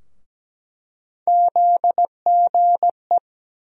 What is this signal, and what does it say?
Morse code, heard as a keyed tone, about 17 wpm, 700 Hz. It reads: ZGE
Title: USA Call Sign Postfixes - 3 Letters 17wpm